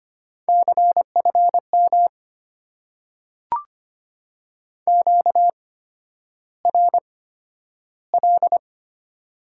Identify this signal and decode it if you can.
Morse code, heard as a keyed tone, about 25 wpm, spaced as for 11 wpm, 690 Hz. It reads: CFM QRL
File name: Top 90 QSO Elements - 8x Character Spacing - 25wpm